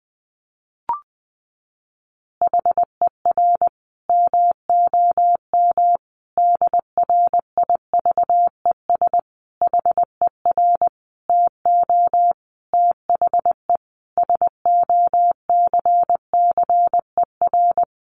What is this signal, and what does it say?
Morse code, heard as a keyed tone, about 20 wpm, 700 Hz. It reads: HER MOM DRIVES HER TO THE SOCCER
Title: Easy Phrases 20wpm